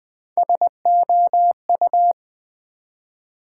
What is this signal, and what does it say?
Morse code, heard as a keyed tone, about 20 wpm, 700 Hz. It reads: SOU
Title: Build A Word - 1 Letter At A Time - 5-Letter Words - 20wpm